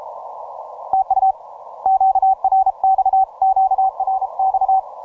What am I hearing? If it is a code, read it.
Morse code, heard as a keyed tone, about 38 wpm, 770 Hz. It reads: TU QRX QRX